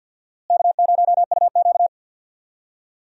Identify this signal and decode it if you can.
Morse code, heard as a keyed tone, about 52 wpm, 700 Hz. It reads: K0UQ
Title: USA Call Signs - 4 Characters - Repeated 3x - 50wpm